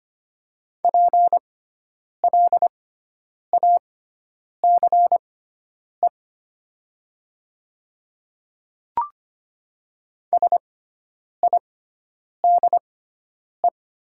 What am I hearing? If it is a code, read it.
Morse code, heard as a keyed tone, about 25 wpm, spaced as for 9 wpm, 700 Hz. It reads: PLACE SIDE
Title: 6x Character Spacing - Top 200 Words - 25wpm